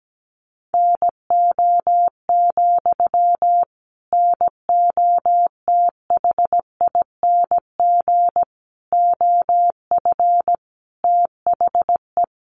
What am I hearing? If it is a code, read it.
Morse code, heard as a keyed tone, about 17 wpm, 700 Hz. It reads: NO, NOTHING OF THE
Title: Sentences from Top 500 Words 17wpm